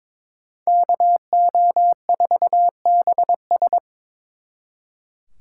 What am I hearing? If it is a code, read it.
Morse code, heard as a keyed tone, about 26 wpm, 700 Hz. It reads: KO4BS
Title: Speed Racing - USA 2x2 Call Signs 22wpm